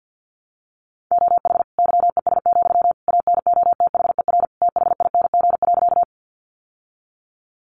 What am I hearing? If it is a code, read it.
Morse code, heard as a keyed tone, about 49 wpm, 700 Hz. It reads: OH YES, ANOTHER THING.
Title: Sentences from Top 600 Words 50wpm